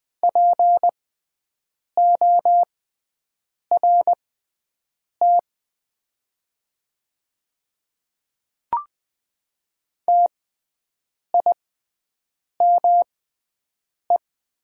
Morse code, heard as a keyed tone, about 20 wpm, spaced as for 7 wpm, 700 Hz. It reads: PORT TIME
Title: 6x Character Spacing - Top 300 Words - 20wpm